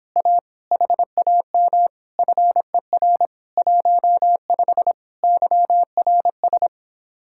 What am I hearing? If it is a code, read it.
Morse code, heard as a keyed tone, about 26 wpm, 700 Hz. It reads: A HAM FER 15 YRS